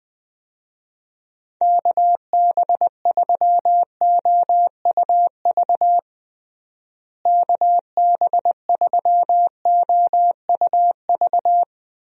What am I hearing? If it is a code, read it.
Morse code, heard as a keyed tone, about 20 wpm, 700 Hz. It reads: KB3OUV KB3OUV
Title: USA Call Signs - 6 Characters - Repeated 2x - 20wpm